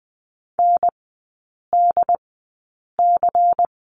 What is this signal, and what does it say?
Morse code, heard as a keyed tone, about 20 wpm, spaced as for 8 wpm, 700 Hz. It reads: NDC
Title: 093 Lesson 16e - Letter c - Three Characters 20wpm